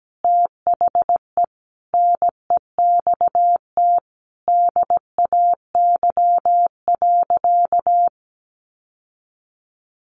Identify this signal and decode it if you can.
Morse code, heard as a keyed tone, about 17 wpm, 700 Hz. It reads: THE NEXT DAY.
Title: Sentences from Top 1200 Words 17wpm